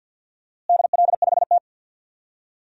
Morse code, heard as a keyed tone, about 47 wpm, 700 Hz. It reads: DL5T
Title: DXCC-100 Call Signs - 4 Characters - Repeated 3x - 50wpm